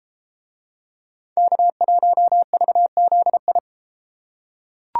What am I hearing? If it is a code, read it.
Morse code, heard as a keyed tone, about 29 wpm, 700 Hz. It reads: K1VZI